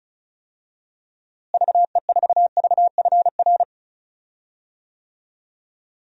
Morse code, heard as a keyed tone, about 36 wpm, 700 Hz. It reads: VE4VFR
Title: DXCC-100 Call Signs - 6 Characters - Repeated 2x - 35wpm